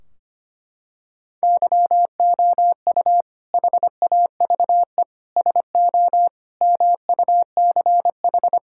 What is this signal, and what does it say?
Morse code, heard as a keyed tone, about 25 wpm, 700 Hz. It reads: YOU HAVE SO MUCH
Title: Sentences from Top 1600 Words 25wpm